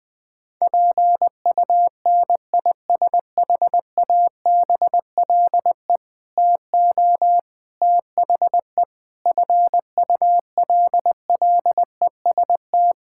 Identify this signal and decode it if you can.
Morse code, heard as a keyed tone, about 20 wpm, 700 Hz. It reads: PUNISHABLE TO THE FULLEST
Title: Sentences with 7-Word Phrases - 20wpm